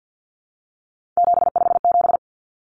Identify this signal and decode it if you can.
Morse code, heard as a keyed tone, about 50 wpm, 700 Hz. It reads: 757
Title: Common Cut Numbers - 3-Digit 50wpm